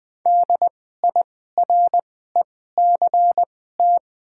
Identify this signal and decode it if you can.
Morse code, heard as a keyed tone, about 20 wpm, spaced as for 14 wpm, 700 Hz. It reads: DIRECT
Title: 2x Character Spacing - Top 500 Words - 20wpm